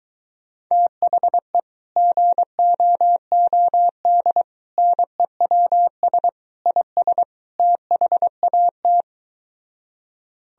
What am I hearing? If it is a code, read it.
Morse code, heard as a keyed tone, about 23 wpm, 700 Hz. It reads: THE GOOD NEWS IS THAT